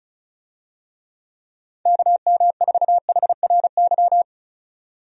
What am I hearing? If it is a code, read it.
Morse code, heard as a keyed tone, about 36 wpm, 700 Hz. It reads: KM4HRY
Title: USA Call Signs - 6 Characters - 35wpm